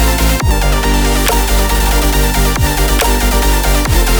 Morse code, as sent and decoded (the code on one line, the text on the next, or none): none